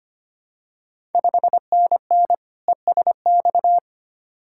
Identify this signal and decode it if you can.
Morse code, heard as a keyed tone, about 25 wpm, 700 Hz. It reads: 5NN ESX
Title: Ontario QSO Party Exchange 25wpm